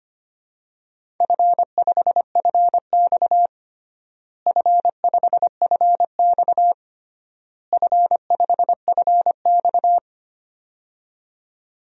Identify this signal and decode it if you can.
Morse code, heard as a keyed tone, about 25 wpm, 700 Hz. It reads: F5FX F5FX F5FX
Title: DXCC-100 Call Signs - 4 Characters - Repeated 3x - 25wpm